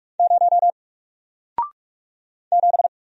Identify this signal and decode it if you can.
Morse code, heard as a keyed tone, about 44 wpm, 700 Hz. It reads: 0 7
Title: Single Letter-Number 45wpm